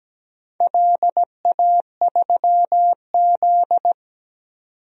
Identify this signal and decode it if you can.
Morse code, heard as a keyed tone, about 17 wpm, 700 Hz. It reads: LA3Z
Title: DXCC-100 Call Signs - 4 Characters - 17wpm